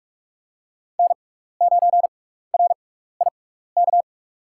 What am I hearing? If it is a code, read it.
Morse code, heard as a keyed tone, about 46 wpm, spaced as for 15 wpm, 700 Hz. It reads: N9RIK
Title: USA Call Signs - 6x Character Spacing - 45wpm